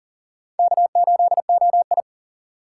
Morse code, heard as a keyed tone, about 51 wpm, 700 Hz. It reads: K8OI